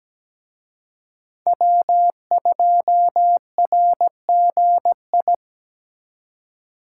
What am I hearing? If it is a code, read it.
Morse code, heard as a keyed tone, about 20 wpm, 700 Hz. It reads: W2RGI
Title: Speed Racing - USA 1x3 Call Signs 17wpm